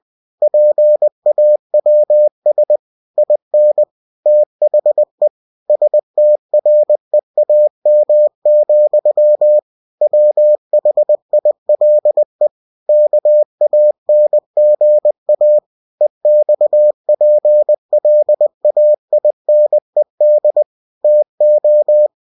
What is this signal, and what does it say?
Morse code, heard as a keyed tone, about 20 wpm, 600 Hz. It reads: PAWS IN THE STREAM, WHILE KANGA EXPLAINED TO
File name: cw-books / winnie-the-pooh / winnie-20wpm-page-0085